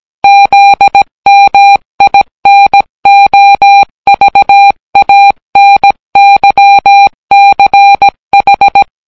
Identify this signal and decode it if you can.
Morse code, heard as a keyed tone, about 17 wpm, 800 Hz. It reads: ZMINOVANYCH